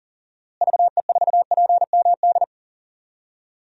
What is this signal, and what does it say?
Morse code, heard as a keyed tone, about 40 wpm, 700 Hz. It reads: VE4PMD
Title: DXCC-100 Call Signs - 6 Characters - Repeated 3x - 40wpm